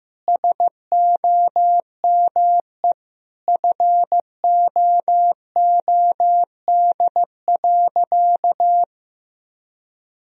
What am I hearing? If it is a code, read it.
Morse code, heard as a keyed tone, about 15 wpm, 700 Hz. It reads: SOME FOOD.
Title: Sentences from Top 1000 Words - Review Part 1 - 15wpm